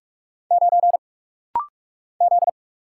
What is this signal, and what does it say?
Morse code, heard as a keyed tone, about 46 wpm, 700 Hz. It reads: 9 Z
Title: Single Letter-Number Alt - Rapid-Fire 45wpm